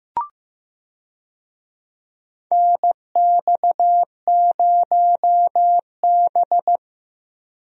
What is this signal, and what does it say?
Morse code, heard as a keyed tone, about 15 wpm, 700 Hz. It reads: NX0B
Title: USA Call Signs - 4 Characters - Repeated 3x - 15wpm